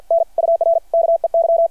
Morse code, heard as a keyed tone, about 53 wpm, 690 Hz. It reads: N UR KEY
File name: archived / CRSnet